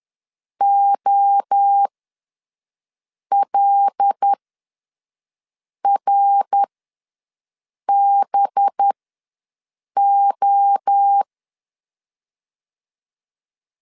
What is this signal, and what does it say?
Morse code, heard as a keyed tone, about 10 wpm, spaced as for 5 wpm, 800 Hz. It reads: OLRBO
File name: LEZIONI / 5 LBVC / 5 LBV ws 10wpm 3645